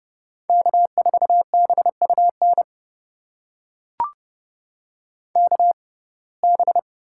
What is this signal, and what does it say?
Morse code, heard as a keyed tone, about 32 wpm, spaced as for 22 wpm, 700 Hz. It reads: K4BUN KB
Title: USA Call Signs - 6x Character Spacing - 30wpm